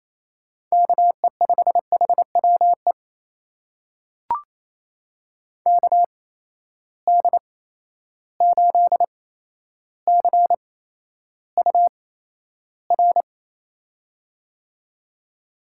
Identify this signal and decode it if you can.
Morse code, heard as a keyed tone, about 28 wpm, spaced as for 12 wpm, 700 Hz. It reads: KE5HWE KD8CUR
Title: USA Call Signs - 6 Characters - 8x Character Spacing - 28wpm